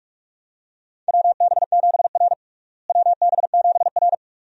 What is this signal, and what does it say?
Morse code, heard as a keyed tone, about 46 wpm, 700 Hz. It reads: WB7R WB7R
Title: USA Call Signs - 4 Characters - Repeated 2x - 45wpm